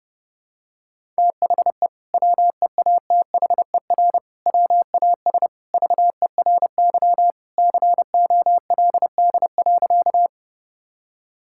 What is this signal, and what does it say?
Morse code, heard as a keyed tone, about 30 wpm, 700 Hz. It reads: THE WEATHER WAS VERY COLD.
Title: Sentences from Top 1000 Words - Review Part 5 - 30wpm